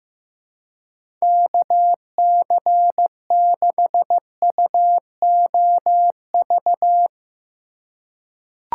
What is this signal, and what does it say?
Morse code, heard as a keyed tone, about 16 wpm, 700 Hz. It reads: KC6UOV
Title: USA Call Signs - 6 Characters - 3x Character Spacing - 15wpm